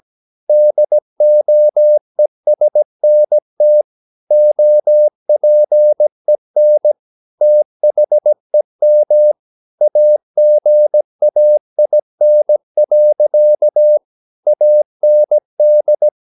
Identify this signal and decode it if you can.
Morse code, heard as a keyed tone, about 17 wpm, 600 Hz. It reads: DOESNT OPEN THEM AGAIN. AND
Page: Romanen Never af Ken Follet på engelsk.